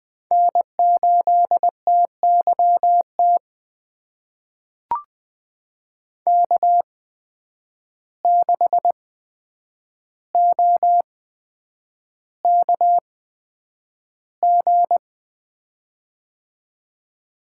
Morse code, heard as a keyed tone, about 20 wpm, spaced as for 9 wpm, 700 Hz. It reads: N8TYT K6OKG
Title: USA Call Signs - 8x Character Spacing - 20wpm